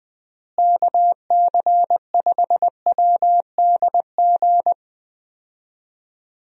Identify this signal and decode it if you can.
Morse code, heard as a keyed tone, about 20 wpm, 700 Hz. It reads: KC5WDG
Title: USA Call Signs - 6 Characters - Repeated 3x - 20wpm